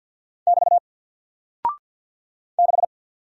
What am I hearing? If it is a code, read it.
Morse code, heard as a keyed tone, about 48 wpm, 700 Hz. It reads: <BT> 6
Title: Single Character (Letter:Number:Punctuation:Prosign) Alt 50wpm